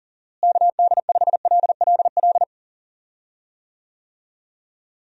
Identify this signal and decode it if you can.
Morse code, heard as a keyed tone, about 37 wpm, 700 Hz. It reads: KD5LLL
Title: USA Call Signs - 6 Characters - Repeated 3x - 40wpm